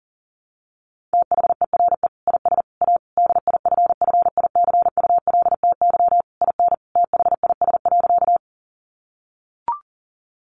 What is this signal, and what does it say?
Morse code, heard as a keyed tone, about 37 wpm, 700 Hz. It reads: THERE IS A DIFFICULTY IN THIS.
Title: Sentences from Top 1500 Words 40wpm